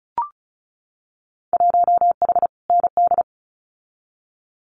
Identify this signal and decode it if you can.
Morse code, heard as a keyed tone, about 36 wpm, 700 Hz. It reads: 1H ND
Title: Winter Field Day Exchanges 35wpm